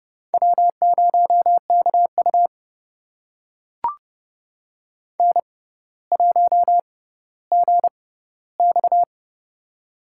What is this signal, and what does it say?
Morse code, heard as a keyed tone, about 30 wpm, spaced as for 15 wpm, 700 Hz. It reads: W0KU N1GX
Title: USA Call Signs - 6x Character Spacing - 30wpm